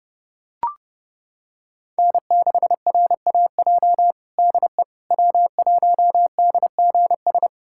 Morse code, heard as a keyed tone, about 30 wpm, 700 Hz. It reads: N6RAJ DE W1DGS
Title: USA Callsign DE Callsign - 5 Characters - 30wpm